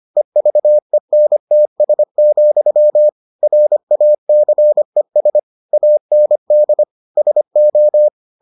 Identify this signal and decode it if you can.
Morse code, heard as a keyed tone, about 25 wpm, 600 Hz. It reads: EVENTS, RACES AND SO